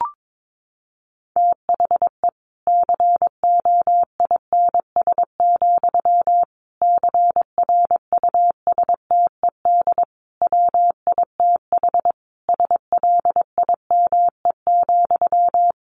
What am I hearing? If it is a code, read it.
Morse code, heard as a keyed tone, about 22 wpm, 700 Hz. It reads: THE COINS, CRUSTED WITH SLIME,
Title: A Sound Of Thunder 22wpm